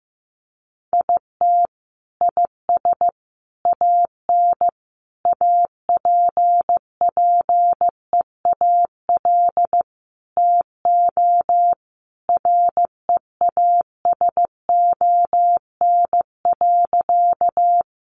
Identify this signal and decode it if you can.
Morse code, heard as a keyed tone, about 15 wpm, 700 Hz. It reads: IT IS AN APPEAL TO REASON.
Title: Sentences from Top 1300 Words 15wpm